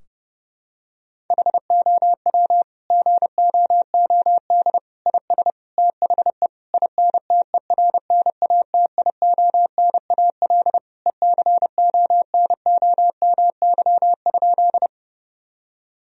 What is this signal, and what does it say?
Morse code, heard as a keyed tone, about 30 wpm, 700 Hz. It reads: HOW GOOD IS THE INTERNATIONAL ECONOMY?
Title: Sentences from Top 1000 Words - Review Part 4 - 30wpm